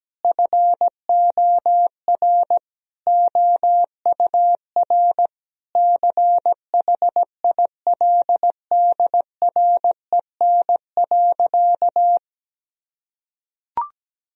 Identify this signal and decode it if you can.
Morse code, heard as a keyed tone, about 17 wpm, 700 Hz. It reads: FOR OUR CHILDREN.
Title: Sentences from Top 1000 Words - Review Part 4 - 17wpm